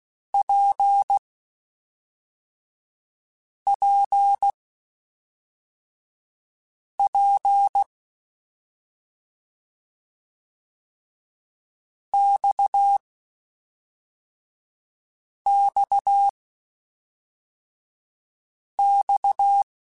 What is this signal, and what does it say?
Morse code, heard as a keyed tone, about 16 wpm, spaced as for 3 wpm, 800 Hz. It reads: PPP XXX
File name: Short Runs / Short Runs-mp3s / Balanced @5